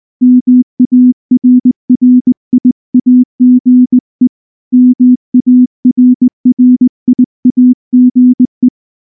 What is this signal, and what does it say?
Morse code, heard as a keyed tone, about 19 wpm, 250 Hz. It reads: MARRIAGE MARRIAGE
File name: morse / morse-training-20WPM-4